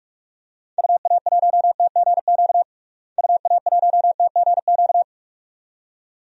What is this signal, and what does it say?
Morse code, heard as a keyed tone, about 46 wpm, 700 Hz. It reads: UA1TGQ UA1TGQ
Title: DXCC-100 Call Signs - 6 Characters - Repeated 2x - 45wpm